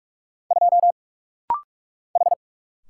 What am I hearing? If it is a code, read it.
Morse code, heard as a keyed tone, about 46 wpm, 700 Hz. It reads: 2 H
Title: Single Letter-Number - Rapid-Fire 45wpm